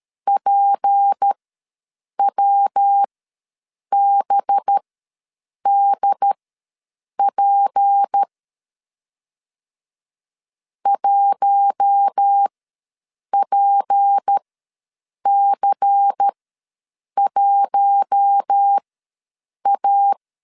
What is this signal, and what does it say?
Morse code, heard as a keyed tone, about 13 wpm, spaced as for 7 wpm, 800 Hz. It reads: PWBDP 1PC1A
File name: LEZIONI / 6 PKX0 / 6 P ws 12wpm 3117